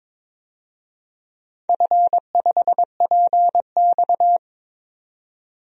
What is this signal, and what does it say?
Morse code, heard as a keyed tone, about 22 wpm, 700 Hz. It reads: F5PX
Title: DXCC-100 Call Signs - 4 Characters - 22wpm